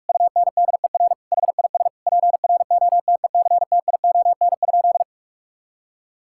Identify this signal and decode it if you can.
Morse code, heard as a keyed tone, about 46 wpm, 700 Hz. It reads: UNDER HIS PROTECTION?
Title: Sentences from Top 1600 Words 45wpm